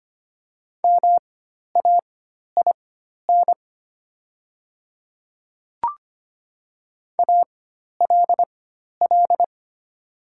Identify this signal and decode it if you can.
Morse code, heard as a keyed tone, about 25 wpm, spaced as for 12 wpm, 700 Hz. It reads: MAIN ALL
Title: 4x Character Spacing - Top 500 Words - 25wpm